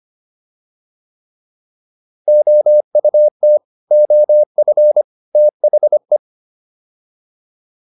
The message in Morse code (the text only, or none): OUT OF THE